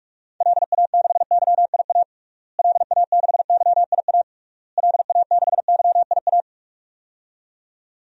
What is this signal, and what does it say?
Morse code, heard as a keyed tone, about 41 wpm, 700 Hz. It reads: LA6YIA LA6YIA LA6YIA
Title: DXCC-100 Call Signs - 6 Characters - Repeated 3x - 45wpm